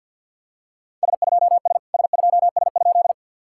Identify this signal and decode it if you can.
Morse code, heard as a keyed tone, about 50 wpm, 700 Hz. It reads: S2S S2S?
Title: SOTA QSO Practice 50wpm